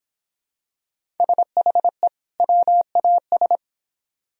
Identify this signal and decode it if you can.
Morse code, heard as a keyed tone, about 26 wpm, 700 Hz. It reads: SHE WAS